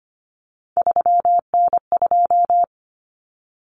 Morse code, heard as a keyed tone, about 25 wpm, 700 Hz. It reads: 3N2
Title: Common Cut Numbers - 3-Digit 25wpm